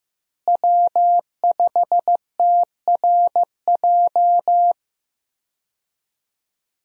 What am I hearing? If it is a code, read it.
Morse code, heard as a keyed tone, about 15 wpm, 700 Hz. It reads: W5TRJ
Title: USA Call Signs - 5 Characters - Repeated 3x - 15wpm